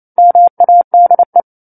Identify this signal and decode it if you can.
Morse code, heard as a keyed tone, about 28 wpm, 700 Hz. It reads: MADE
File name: 09-Chapter-09 28 WPM